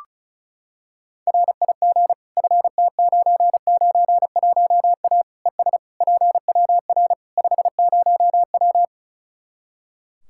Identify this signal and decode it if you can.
Morse code, heard as a keyed tone, about 34 wpm, 700 Hz. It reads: RIG FT991A ES PWR 50W
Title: Practice QSO - Exchange 2 - Info - 35wpm